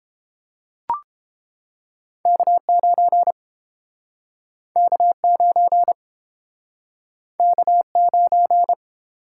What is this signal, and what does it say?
Morse code, heard as a keyed tone, about 31 wpm, 700 Hz. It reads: K9 K9 K9